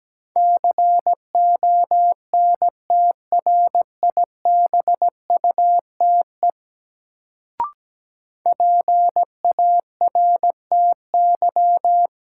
Morse code, heard as a keyed tone, about 17 wpm, 700 Hz. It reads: CONTRIBUTE PARTY
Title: Sentences from Top 1400 Words - New Word Drill Alt - 17wpm